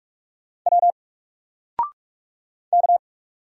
Morse code, heard as a keyed tone, about 46 wpm, 700 Hz. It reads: W K
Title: Single Letter-Number 45wpm